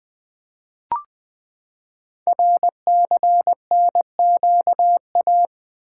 Morse code, heard as a keyed tone, about 20 wpm, 700 Hz. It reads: RCNQA
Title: Code Groups - 5 Letters - 20wpm